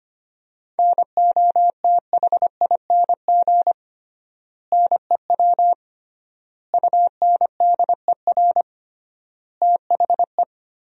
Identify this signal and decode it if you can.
Morse code, heard as a keyed tone, about 25 wpm, 700 Hz. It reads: NOTHING NEW UNDER THE
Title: 3x Word-Spacing - Sentences from Top 500 Words - 25wpm